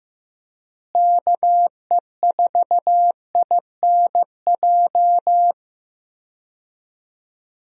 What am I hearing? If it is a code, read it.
Morse code, heard as a keyed tone, about 15 wpm, 700 Hz. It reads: KE4INJ
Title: USA Call Signs - 6 Characters - Repeated 3x - 15wpm